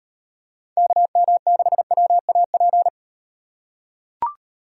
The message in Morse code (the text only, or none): KM6WAP